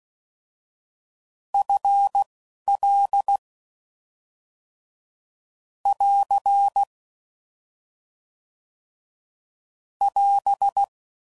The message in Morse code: FL <AR> <AS>

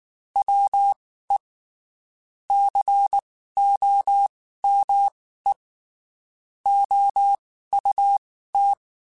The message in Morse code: WE COME OUT

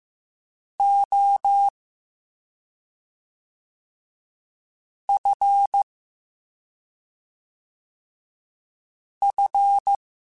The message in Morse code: O F F

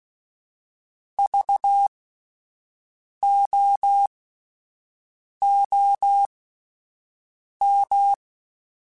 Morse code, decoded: VOOM

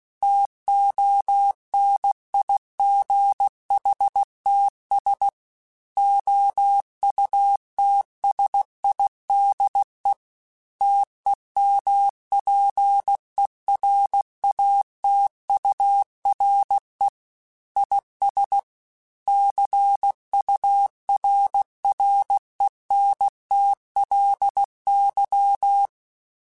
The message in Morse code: TONIGHTS OUTSIDE TEMPERATURE IS CURRENTLY

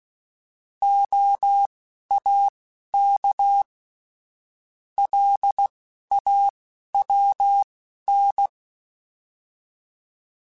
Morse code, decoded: OAK LAWN

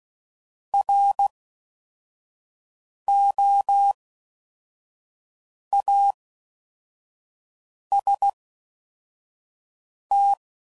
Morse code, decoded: ROAST

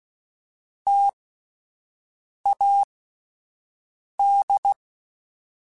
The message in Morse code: TAD

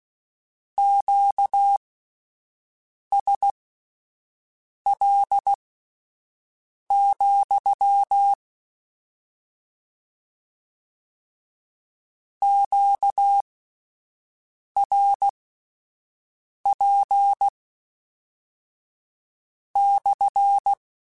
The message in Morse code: QSL, QRP /

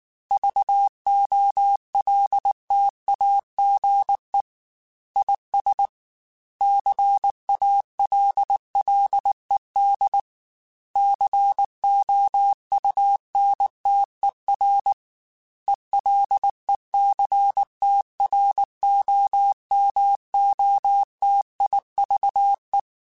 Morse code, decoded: VOLTAGE IS CALLED COUNTER ELECTROMOTIVE

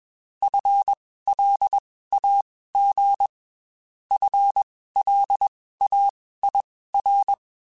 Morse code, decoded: FLAG FLAIR

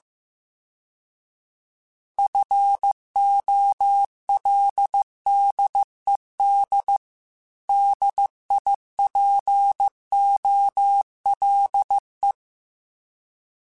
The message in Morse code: FOLDED DIPOLE

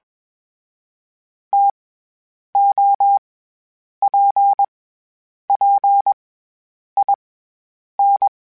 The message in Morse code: TOPPIN

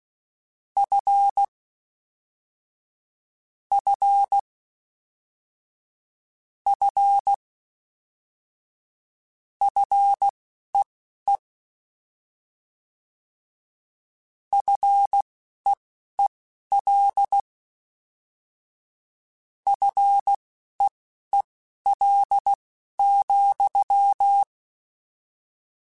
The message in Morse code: F F F FEE FEEL FEEL,

